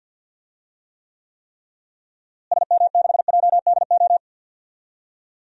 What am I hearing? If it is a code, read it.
Morse code, heard as a keyed tone, about 46 wpm, 700 Hz. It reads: SM6JDO
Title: DXCC-100 Call Signs - 6 Characters - Repeated 3x - 50wpm